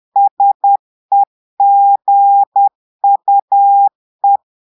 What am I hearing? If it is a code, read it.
Morse code, heard as a keyed tone, about 10 wpm, 800 Hz. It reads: SEGUE
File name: Portugais - texte en clair 1 - vitesse - 10 mots minute